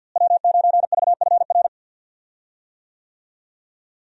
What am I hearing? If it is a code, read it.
Morse code, heard as a keyed tone, about 52 wpm, 700 Hz. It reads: W9VFR
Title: USA Call Signs - 5 Characters - Repeated 3x - 50wpm